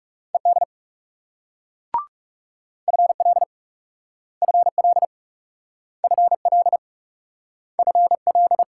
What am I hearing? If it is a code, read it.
Morse code, heard as a keyed tone, about 38 wpm, 700 Hz. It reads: ED FL FL FL FL